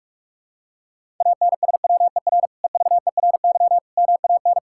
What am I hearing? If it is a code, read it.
Morse code, heard as a keyed tone, about 45 wpm, 700 Hz. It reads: ANSWER EVERY MAN
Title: Sentences from Top 1000 Words - Review Part 5 - 45wpm